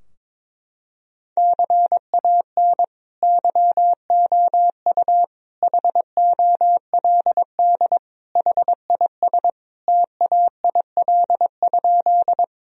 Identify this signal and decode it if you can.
Morse code, heard as a keyed tone, about 22 wpm, 700 Hz. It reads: CAN YOU HOLD HIS TAIL?
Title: Sentences from Top 1000 Words - Review Part 3 - 22wpm